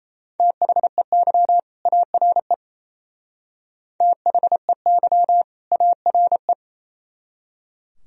Morse code, heard as a keyed tone, about 34 wpm, 700 Hz. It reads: THEY ARE THEY ARE
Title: Speed-Racing - Sets of 2 Words 28wpm